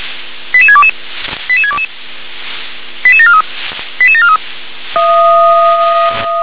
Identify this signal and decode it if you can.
Five-tone selective call (ZVEI), heard as quick runs of five tones, tones of about 70 ms. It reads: 88622 88622 88743 88743